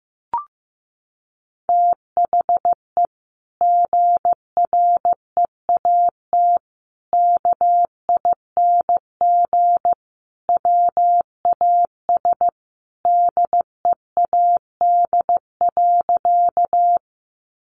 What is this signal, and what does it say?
Morse code, heard as a keyed tone, about 15 wpm, 700 Hz. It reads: THE GREAT KING WAS DEAD.
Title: Sentences from Top 1500 Words 15wpm